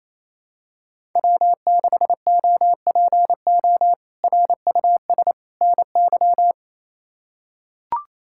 Morse code, heard as a keyed tone, about 28 wpm, 700 Hz. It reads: W6OPO RUS NY
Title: K1USN SST QSO Practice - 28wpm